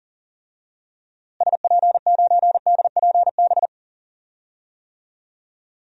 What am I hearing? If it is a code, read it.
Morse code, heard as a keyed tone, about 40 wpm, 700 Hz. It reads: SP9DPB
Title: DXCC-100 Call Signs - 6 Characters - Repeated 3x - 40wpm